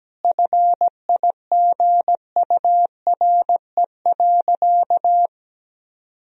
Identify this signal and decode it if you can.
Morse code, heard as a keyed tone, about 17 wpm, 700 Hz. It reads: FIGURE.
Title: Sentences from Top 1000 Words - Review Part 2 - 17wpm